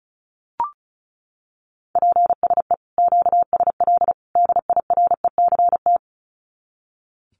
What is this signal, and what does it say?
Morse code, heard as a keyed tone, about 34 wpm, 700 Hz. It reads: PSE QSL DIRECT
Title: Common QSO Phrases 35wpm